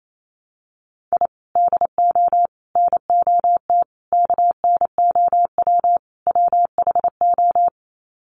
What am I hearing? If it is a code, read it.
Morse code, heard as a keyed tone, about 28 wpm, 700 Hz. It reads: I DO NOT KNOW WHO
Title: Sets of 5 Words 28wpm